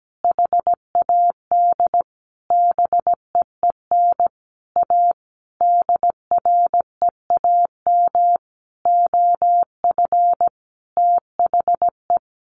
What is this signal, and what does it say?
Morse code, heard as a keyed tone, about 17 wpm, 700 Hz. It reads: HAD BEEN A DREAM OF THE
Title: Sentences from Top 1500 Words 17wpm